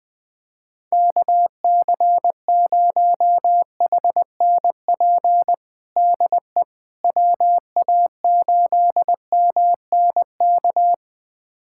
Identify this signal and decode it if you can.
Morse code, heard as a keyed tone, about 21 wpm, 700 Hz. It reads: KC0HNP DE WA8MNK
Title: USA Callsign DE Callsign 20wpm